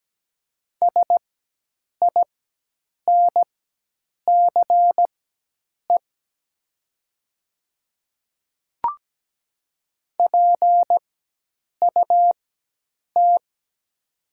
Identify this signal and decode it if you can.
Morse code, heard as a keyed tone, about 17 wpm, spaced as for 8 wpm, 700 Hz. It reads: SINCE PUT
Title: 4x Character Spacing - Top 500 Words - 17wpm